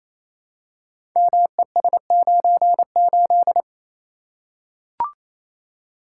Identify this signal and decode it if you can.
Morse code, heard as a keyed tone, about 27 wpm, 700 Hz. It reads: MES98
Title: Code Groups - 5 Letter:Numbers - 28wpm